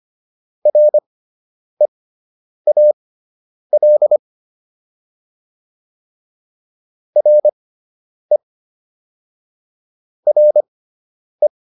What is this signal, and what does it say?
Morse code, heard as a keyed tone, about 25 wpm, spaced as for 9 wpm, 600 Hz. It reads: REAL RE RE